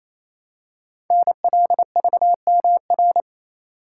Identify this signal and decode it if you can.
Morse code, heard as a keyed tone, about 28 wpm, 700 Hz. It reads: NLVMR
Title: Code Groups - 5 Letters - 28wpm